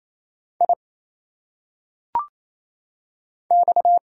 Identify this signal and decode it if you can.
Morse code, heard as a keyed tone, about 27 wpm, 700 Hz. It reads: I X
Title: Single Letters - Phonetic Pronunciation 28wpm